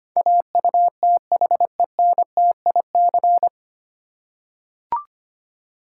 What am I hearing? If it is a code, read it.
Morse code, heard as a keyed tone, about 25 wpm, 700 Hz. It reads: AUTHENTIC
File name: practice / 25wpm / 9 Letter Words - First 3 Letters Given - 25wpm